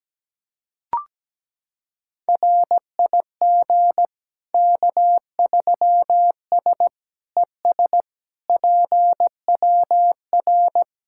Morse code, heard as a keyed tone, about 17 wpm, 700 Hz. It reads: RIG K3S ES PWR
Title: Practice QSO - Exchange 2 - Info - 17wpm